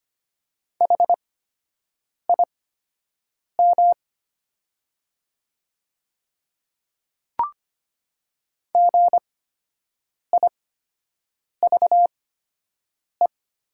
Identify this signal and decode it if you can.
Morse code, heard as a keyed tone, about 25 wpm, spaced as for 7 wpm, 700 Hz. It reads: HIM GIVE
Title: 8x Character Spacing - Top 300 Words - 25wpm